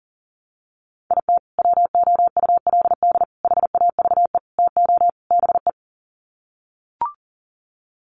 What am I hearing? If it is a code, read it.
Morse code, heard as a keyed tone, about 40 wpm, 700 Hz. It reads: IT WOULD HAVE TO BE
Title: Sets of 5 Words Encore 40wpm